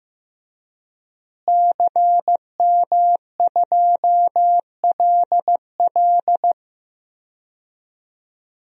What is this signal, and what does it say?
Morse code, heard as a keyed tone, about 15 wpm, 700 Hz. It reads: CM2LL
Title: DXCC-100 Call Signs - 5 Characters - Repeated 2x - 15wpm